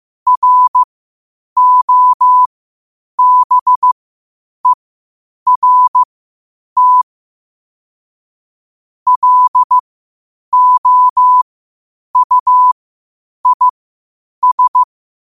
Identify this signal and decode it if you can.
Morse code, heard as a keyed tone, about 15 wpm, spaced as for 8 wpm, 1000 Hz. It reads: ROBERT LOUIS
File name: qotdfiles / 8wpm